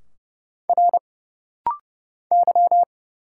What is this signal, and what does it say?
Morse code, heard as a keyed tone, about 31 wpm, 700 Hz. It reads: R Y
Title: Single Letter-Number Alt - Rapid-Fire 30wpm